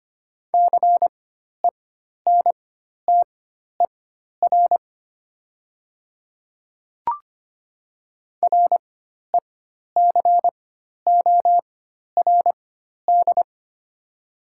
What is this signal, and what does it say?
Morse code, heard as a keyed tone, about 25 wpm, spaced as for 12 wpm, 700 Hz. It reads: CENTER RECORD
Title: 4x Character Spacing - Top 500 Words - 25wpm